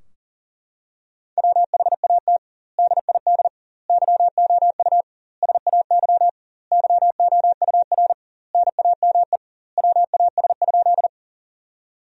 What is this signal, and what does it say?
Morse code, heard as a keyed tone, about 40 wpm, 700 Hz. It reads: WHAT DID YOU SAY YOUR NAME WAS?
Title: 2x Word-Spacing - Sentences from Top 200 Words - 40wpm